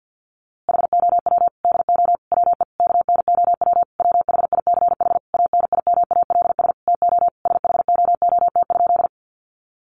Morse code, heard as a keyed tone, about 50 wpm, 700 Hz. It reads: HOW DO WE KNOW WHICH ANIMALS TO SHOOT?
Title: A Sound Of Thunder 50wpm